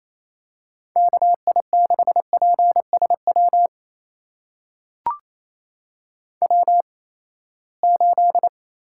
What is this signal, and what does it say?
Morse code, heard as a keyed tone, about 28 wpm, spaced as for 20 wpm, 700 Hz. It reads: KI6PSW W8
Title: USA Call Signs - 8x Character Spacing - 28wpm